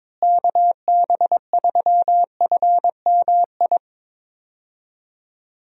Morse code, heard as a keyed tone, about 22 wpm, 700 Hz. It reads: KB3FMI
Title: USA Call Signs - 6 Characters - Repeated 2x - 22wpm